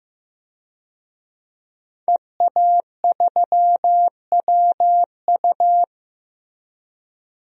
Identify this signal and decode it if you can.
Morse code, heard as a keyed tone, about 15 wpm, 700 Hz. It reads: EA3WU
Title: DXCC-100 Call Signs - 5 Characters - 15wpm